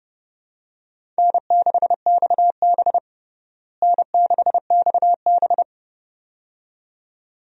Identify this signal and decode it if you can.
Morse code, heard as a keyed tone, about 30 wpm, 700 Hz. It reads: N6XB N6XB
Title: USA Call Signs - 4 Characters - Repeated 2x - 30wpm